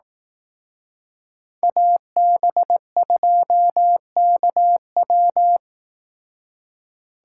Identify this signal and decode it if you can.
Morse code, heard as a keyed tone, about 18 wpm, 700 Hz. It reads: AB2KW
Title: Speed Racing - USA 2x2 Call Signs 15wpm